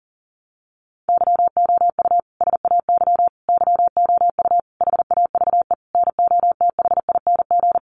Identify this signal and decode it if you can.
Morse code, heard as a keyed tone, about 40 wpm, 700 Hz. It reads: YOU SAY YOU HAVE NOTHING
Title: Sentences from Top 700 Words 40wpm